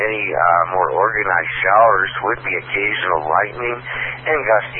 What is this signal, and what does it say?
Emergency Alert System (SAME) tones, heard as harsh buzzing bursts, with speech. It is heard from 1.2 to 4.2 s.